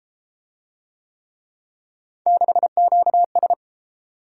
Morse code, heard as a keyed tone, about 31 wpm, 700 Hz. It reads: 6QS